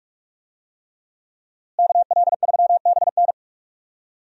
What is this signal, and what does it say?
Morse code, heard as a keyed tone, about 45 wpm, 700 Hz. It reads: KL3BN
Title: USA Call Signs - 5 Characters - Repeated 3x - 45wpm